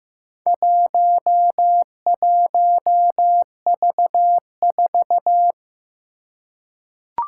Morse code, heard as a keyed tone, about 15 wpm, 700 Hz. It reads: 11V4
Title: Four Challenging Letter-Number Combinations 15wpm